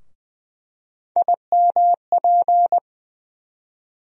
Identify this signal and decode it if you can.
Morse code, heard as a keyed tone, about 20 wpm, 700 Hz. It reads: IMP
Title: USA Call Sign Postfixes - 3 Letters 20wpm